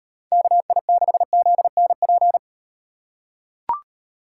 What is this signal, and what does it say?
Morse code, heard as a keyed tone, about 38 wpm, 700 Hz. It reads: KI6ZNP